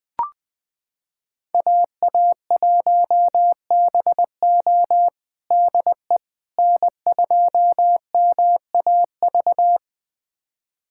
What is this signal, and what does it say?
Morse code, heard as a keyed tone, about 20 wpm, 700 Hz. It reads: AA1BO DE N2MAV
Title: USA Callsign DE Callsign - 5 Characters - 20wpm